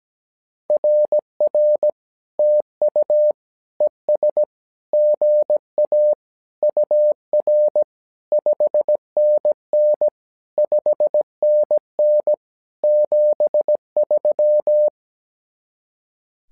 Morse code, heard as a keyed tone, about 17 wpm, 600 Hz. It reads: RR TU ES GA UR 5NN 5NN 73
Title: SOTA QSO Practice 17wpm